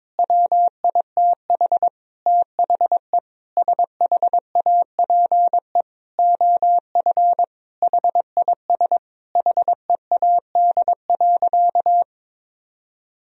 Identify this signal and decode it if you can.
Morse code, heard as a keyed tone, about 22 wpm, 700 Hz. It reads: WITH THE SHAPE OF HIS HEAD.
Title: Sentences from Top 1000 Words - Review Part 3 - 22wpm